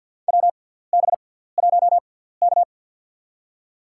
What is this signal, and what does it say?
Morse code, heard as a keyed tone, about 53 wpm, spaced as for 17 wpm, 690 Hz. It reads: WB1K
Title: USA Call Signs - 6x Character Spacing - 50wpm